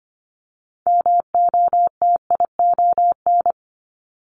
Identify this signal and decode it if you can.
Morse code, heard as a keyed tone, about 25 wpm, 700 Hz. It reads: MOTION
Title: Multi-letter Phonemes Repeated 3x 25wpm